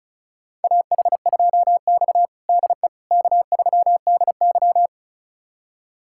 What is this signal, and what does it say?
Morse code, heard as a keyed tone, about 36 wpm, 700 Hz. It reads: AH2X DE K3DY
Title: USA Callsign DE Callsign - 4 Characters - 35wpm